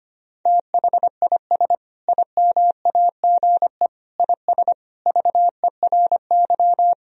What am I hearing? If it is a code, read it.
Morse code, heard as a keyed tone, about 25 wpm, 700 Hz. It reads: THIS IMAGE IS VERY
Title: Sentences from Top 1000 Words - Review Part 4 - 25wpm